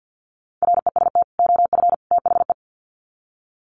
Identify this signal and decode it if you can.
Morse code, heard as a keyed tone, about 50 wpm, 700 Hz. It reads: REST OF THE
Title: Sets of 3 Words 50wpm